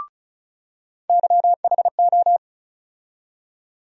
Morse code, heard as a keyed tone, about 34 wpm, 700 Hz. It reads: YHO
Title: USA Call Sign Postfixes - 3 Letters 35wpm